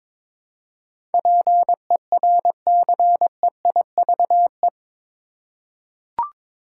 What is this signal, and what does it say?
Morse code, heard as a keyed tone, about 21 wpm, 700 Hz. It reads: PERCEIVE
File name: practice / 22wpm / Eight Letter Words - 22wpm